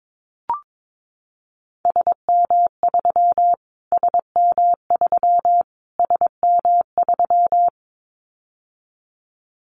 Morse code, heard as a keyed tone, about 22 wpm, 700 Hz. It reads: SM3 SM3 SM3
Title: Callsign Prefixes for Top 25 DXCC - Repeated 3x - 22wpm